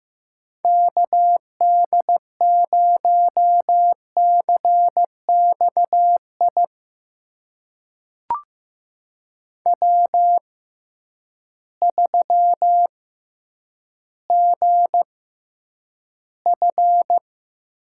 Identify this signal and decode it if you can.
Morse code, heard as a keyed tone, about 15 wpm, spaced as for 9 wpm, 700 Hz. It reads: KD0CXI W3GF
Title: USA Call Signs - 6x Character Spacing - 15wpm